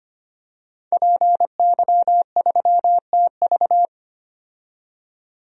Morse code, heard as a keyed tone, about 25 wpm, 700 Hz. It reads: PY3TV
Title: DXCC-100 Call Signs - 5 Characters - 25wpm